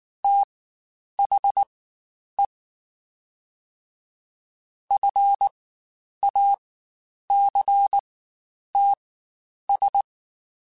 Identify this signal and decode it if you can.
Morse code, heard as a keyed tone, about 19 wpm, spaced as for 9 wpm, 800 Hz. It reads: THE FACTS
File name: Quotes Spaced / Quotes Spaced-mp3s / IgnoringFacts sp @18